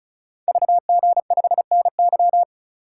Morse code, heard as a keyed tone, about 36 wpm, 700 Hz. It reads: VG5NY
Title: Code Groups - 5 Letter:Numbers - 35wpm